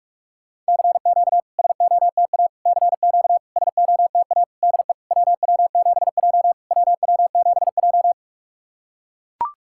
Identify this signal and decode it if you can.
Morse code, heard as a keyed tone, about 42 wpm, 690 Hz. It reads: CQ SOTA CQ SOTA DE WW7J WW7J
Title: SOTA QSO Practice 45wpm